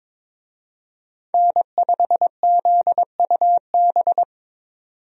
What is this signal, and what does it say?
Morse code, heard as a keyed tone, about 22 wpm, 700 Hz. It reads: N5ZUB
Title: USA Call Signs - 5 Characters - 22wpm